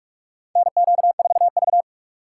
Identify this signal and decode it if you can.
Morse code, heard as a keyed tone, about 46 wpm, 700 Hz. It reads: NQ4V
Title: USA Call Signs - 4 Characters - 3x Character Spacing - 45wpm